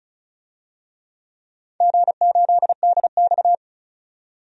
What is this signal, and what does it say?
Morse code, heard as a keyed tone, about 36 wpm, 700 Hz. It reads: G8DX
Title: DXCC-100 Call Signs - 4 Characters - 35wpm